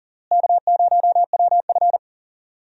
Morse code, heard as a keyed tone, about 40 wpm, 700 Hz. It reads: K0WF
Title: USA Call Signs - 4 Characters - Repeated 3x - 40wpm